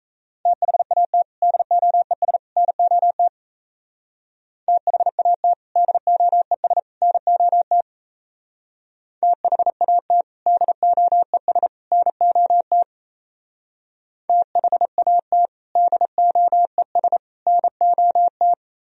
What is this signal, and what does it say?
Morse code, heard as a keyed tone, about 35 wpm, 700 Hz. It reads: THAT DOES NOT THAT DOES NOT THAT DOES NOT THAT DOES NOT
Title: Speed-Racing - Sets of 3 Words 28wpm